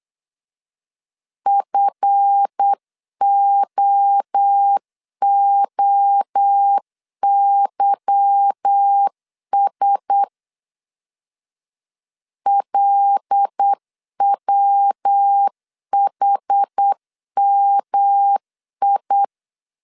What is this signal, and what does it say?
Morse code, heard as a keyed tone, about 8 wpm, 800 Hz. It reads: FOOYS LWHMI
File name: CW / DET / 1 9 1500C S A 8wpm 5637